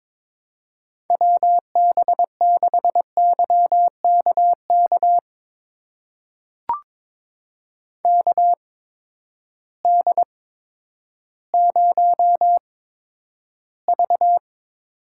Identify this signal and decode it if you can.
Morse code, heard as a keyed tone, about 22 wpm, spaced as for 11 wpm, 700 Hz. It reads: WB6YKK KD0V
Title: USA Call Signs - 6 Characters - 8x Character Spacing - 22wpm